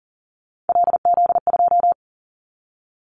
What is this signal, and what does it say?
Morse code, heard as a keyed tone, about 40 wpm, 700 Hz. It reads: LZ2
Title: Callsign Prefixes for Top 50 DXCC - 40wpm